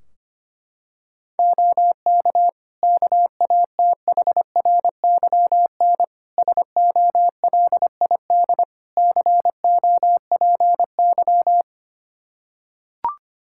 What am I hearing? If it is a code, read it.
Morse code, heard as a keyed tone, about 25 wpm, 700 Hz. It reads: OK KATHRYN SOLID COPY
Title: K1USN SST QSO Practice 25wpm